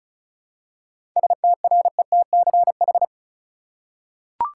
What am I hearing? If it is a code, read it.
Morse code, heard as a keyed tone, about 36 wpm, 700 Hz. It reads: STRETCH
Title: Seven Letter Words - 35wpm